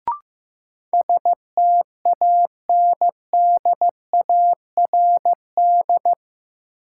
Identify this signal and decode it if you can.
Morse code, heard as a keyed tone, about 15 wpm, 700 Hz. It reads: STANDARD
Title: Sentences from Top 1200 Words - New Word Drill Alt - 15wpm